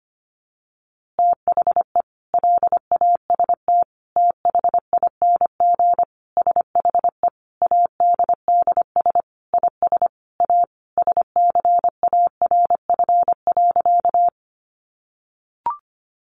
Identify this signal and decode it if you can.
Morse code, heard as a keyed tone, about 25 wpm, 700 Hz. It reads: THE LAST THING SHE ADDS IS A SCARF.
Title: Easy Phrases 25wpm